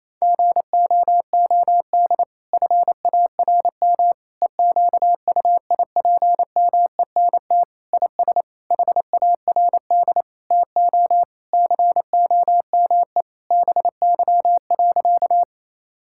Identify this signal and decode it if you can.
Morse code, heard as a keyed tone, about 28 wpm, 700 Hz. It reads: GOOD FARM EQUIPMENT IS HARD TO COME BY.
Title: Sentences from Top 1000 Words - Review Part 5 - 28wpm